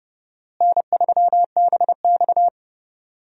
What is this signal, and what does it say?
Morse code, heard as a keyed tone, about 29 wpm, 700 Hz. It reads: N3BX
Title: USA Call Signs - 4 Characters - Repeated 2x - 30wpm